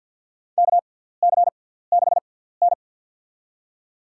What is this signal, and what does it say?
Morse code, heard as a keyed tone, about 51 wpm, spaced as for 17 wpm, 700 Hz. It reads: KC6N
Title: USA Call Signs - 4 Characters - 6x Character Spacing - 50wpm